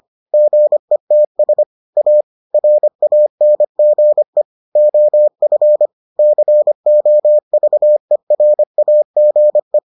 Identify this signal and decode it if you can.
Morse code, heard as a keyed tone, about 25 wpm, 600 Hz. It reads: GETS A RANGE OF COVERAGE